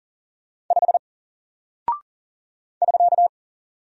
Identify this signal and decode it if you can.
Morse code, heard as a keyed tone, about 39 wpm, 700 Hz. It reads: 5 <SK>
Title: Single Character (Letter:Number:Punctuation:Prosign) Alt 40wpm